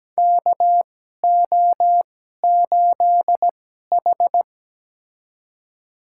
Morse code, heard as a keyed tone, about 17 wpm, spaced as for 12 wpm, 700 Hz. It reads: KO8H
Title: USA Call Signs - 4 Characters - 2x Character Spacing - 17wpm